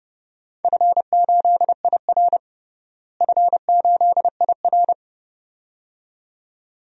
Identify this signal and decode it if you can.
Morse code, heard as a keyed tone, about 30 wpm, 700 Hz. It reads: F8IR F8IR
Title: DXCC-100 Call Signs - 4 Characters - Repeated 2x - 30wpm